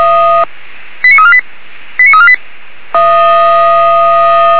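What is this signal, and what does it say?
Five-tone selective call (ZVEI), heard as quick runs of five tones, tones of about 70 ms. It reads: 89348 89348
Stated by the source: ZVEI five-tone selective call received with an SDR.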